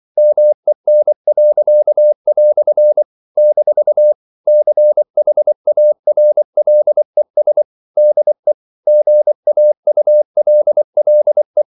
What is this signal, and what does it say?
Morse code, heard as a keyed tone, about 24 wpm, 600 Hz. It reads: MEN." - CHARLES DE GAULLE
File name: CW / Practice / EnglishQuotations0324